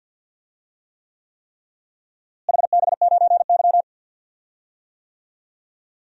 Morse code, heard as a keyed tone, about 53 wpm, 700 Hz. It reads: HB9Y
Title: DXCC-100 Call Signs - 4 Characters - Repeated 3x - 50wpm